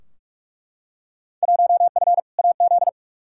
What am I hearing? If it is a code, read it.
Morse code, heard as a keyed tone, about 41 wpm, 700 Hz. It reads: 1F AZ
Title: Field Day Exchanges - 45wpm